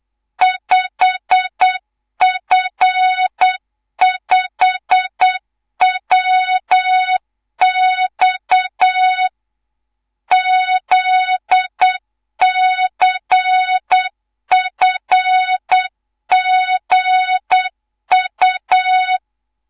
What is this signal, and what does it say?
Morse code, heard as a keyed tone, about 8 wpm, 3700 Hz, with noise. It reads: 5F5WX ZCFGU
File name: Morze-HA2MN / hang / 11 GYAKORLAT